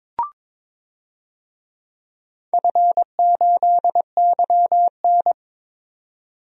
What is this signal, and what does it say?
Morse code, heard as a keyed tone, about 22 wpm, 700 Hz. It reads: F8YN
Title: DXCC-100 Call Signs - 4 Characters - 22wpm